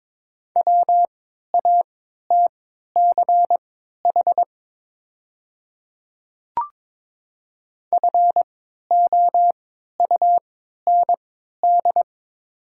Morse code, heard as a keyed tone, about 22 wpm, spaced as for 12 wpm, 700 Hz. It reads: WATCH FOUND
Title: 3x Character Spacing - Top 500 Words - 22wpm